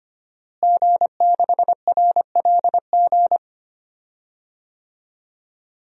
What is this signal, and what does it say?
Morse code, heard as a keyed tone, about 25 wpm, 700 Hz. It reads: G6RLG
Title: DXCC-100 Call Signs - 5 Characters - Repeated 3x - 25wpm